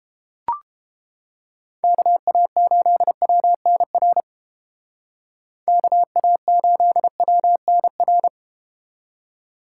Morse code, heard as a keyed tone, about 30 wpm, 700 Hz. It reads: KA8WNR KA8WNR